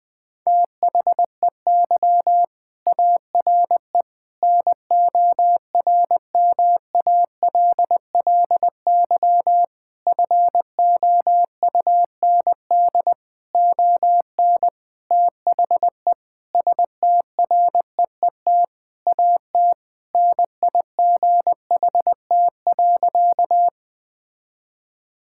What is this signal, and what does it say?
Morse code, heard as a keyed tone, about 21 wpm, 700 Hz. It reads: THEY ARE NORMALLY FOUND ON THE STREET AT NIGHT.
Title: Sentences from Top 1000 Words - Review Part 6 - 20wpm